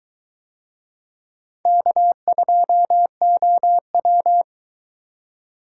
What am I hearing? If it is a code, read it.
Morse code, heard as a keyed tone, about 21 wpm, 700 Hz. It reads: K2OW